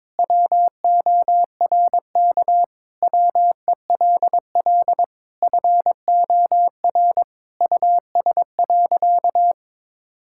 Morse code, heard as a keyed tone, about 22 wpm, 700 Hz. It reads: WORK WELL FOR US.
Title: Sentences from Top 1000 Words - Review Part 6 - 22wpm